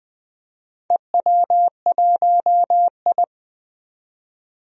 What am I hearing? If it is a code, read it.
Morse code, heard as a keyed tone, about 20 wpm, 700 Hz. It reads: EW1I
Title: DXCC-100 Call Signs - 4 Characters - 20wpm